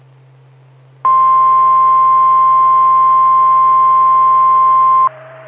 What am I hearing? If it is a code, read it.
Emergency Alert System (SAME) tones (the 1050 Hz attention tone). It is heard from 1.0 to 5.1 s.
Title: Saline, AK SKYWARN Repeater System - W5RHS